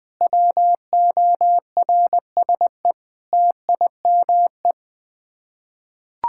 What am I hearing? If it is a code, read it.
Morse code, heard as a keyed tone, about 20 wpm, 700 Hz. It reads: WORSE TIME
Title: Sets of 8 Words - 20wpm